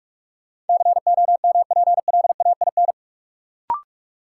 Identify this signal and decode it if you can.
Morse code, heard as a keyed tone, about 46 wpm, 700 Hz. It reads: COMPLAIN
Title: Sentences from Top 1300 Words - New Word Drill Alt - 45wpm